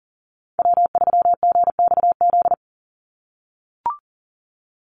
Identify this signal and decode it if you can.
Morse code, heard as a keyed tone, about 40 wpm, 690 Hz. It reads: W3GXZ
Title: USA Call Signs 40wpm